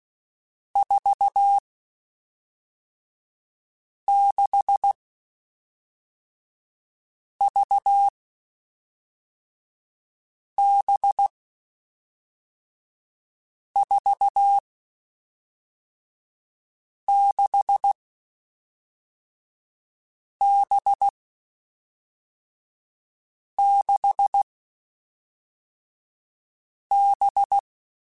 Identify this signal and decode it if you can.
Morse code, heard as a keyed tone, about 16 wpm, 800 Hz. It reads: 4 6 V B 4 6 B 6 B
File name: Groups@5 / Groups@5 mp3s / VB46@5